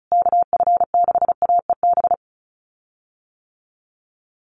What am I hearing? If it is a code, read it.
Morse code, heard as a keyed tone, about 34 wpm, 700 Hz. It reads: KF6AEB
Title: USA Call Signs Repeated 3x 35wpm